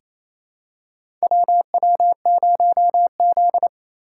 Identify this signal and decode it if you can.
Morse code, heard as a keyed tone, about 28 wpm, 700 Hz. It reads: WW0Z
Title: USA Call Signs - 4 Characters - 4x Character Spacing - 28wpm